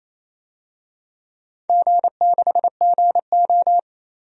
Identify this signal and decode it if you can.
Morse code, heard as a keyed tone, about 28 wpm, 700 Hz. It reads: G6GO
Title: DXCC-100 Call Signs - 4 Characters - 28wpm